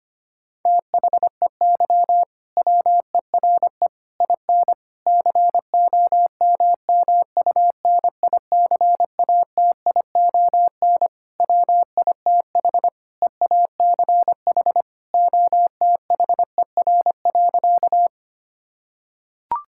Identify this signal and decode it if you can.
Morse code, heard as a keyed tone, about 25 wpm, 700 Hz. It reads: THEY WERE IN COMMUNICATION WITH EACH OTHER.
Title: Sentences from Top 1000 Words - Review Part 6 - 25wpm